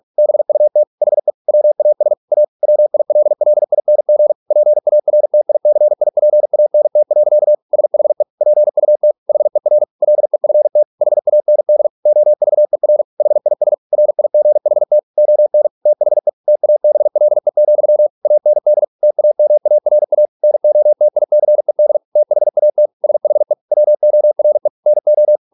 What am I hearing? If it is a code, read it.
Morse code, heard as a keyed tone, about 46 wpm, 600 Hz. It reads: BUT HE WAS A WILLING PARTICIPANT. SHE PUT HER LEFT HAND OVER HIS RIGHT ON THE TABLE, AND TAMARA NOTICED THAT SHE WORE NO